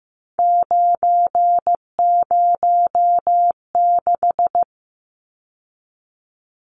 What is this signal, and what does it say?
Morse code, heard as a keyed tone, about 15 wpm, 700 Hz. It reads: 906
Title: Numbers - 3-Digit 15wpm